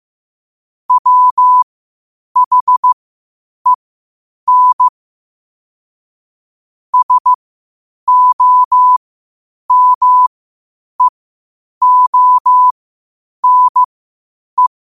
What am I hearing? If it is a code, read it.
Morse code, heard as a keyed tone, about 15 wpm, spaced as for 8 wpm, 1000 Hz. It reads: WHEN SOMEONE
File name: qotdfiles / 8wpm